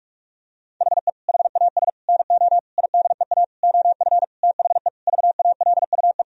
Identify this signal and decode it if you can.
Morse code, heard as a keyed tone, about 41 wpm, 700 Hz. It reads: HE HAS NO IDEA OF THE VALUE
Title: Sentences from Top 1000 Words - Review Part 3 - 45wpm